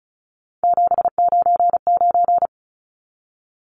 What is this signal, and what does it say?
Morse code, heard as a keyed tone, about 34 wpm, 700 Hz. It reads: 799
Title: Numbers - 3-Digit 35wpm